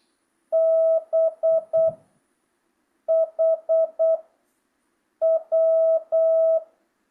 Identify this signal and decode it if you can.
Morse code, heard as a keyed tone, about 8 wpm, 650 Hz. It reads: B H W
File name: jh8cbh / 2025cw